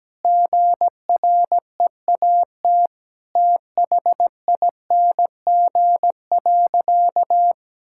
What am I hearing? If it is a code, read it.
Morse code, heard as a keyed tone, about 17 wpm, 700 Hz. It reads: GREAT THING.
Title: Sentences from Top 1000 Words - Review Part 5 - 17wpm